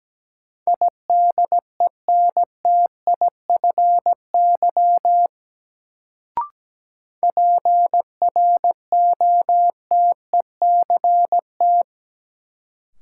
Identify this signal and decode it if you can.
Morse code, heard as a keyed tone, about 17 wpm, 700 Hz. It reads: IDENTIFY PROTECT
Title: Sentences from Top 900 Words - New Word Drill - 17wpm